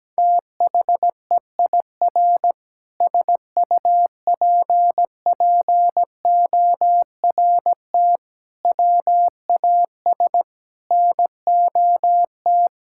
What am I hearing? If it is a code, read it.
Morse code, heard as a keyed tone, about 17 wpm, 700 Hz. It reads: THEIR SUPPORT WAS NOT
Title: Sentences from Top 1000 Words - Review Part 6 - 17wpm